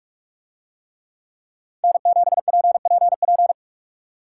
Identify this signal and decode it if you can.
Morse code, heard as a keyed tone, about 45 wpm, 700 Hz. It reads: N7PPP